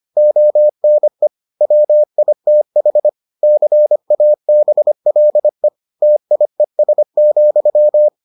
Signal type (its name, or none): Morse code